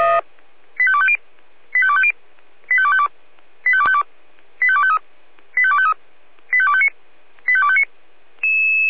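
Five-tone selective call (ZVEI), heard as quick runs of five tones, tones of about 70 ms. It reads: 85250 85250 85252 85252 85253 85253 85259 85259
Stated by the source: ZVEI five-tone selective call 85358 received on an SDR.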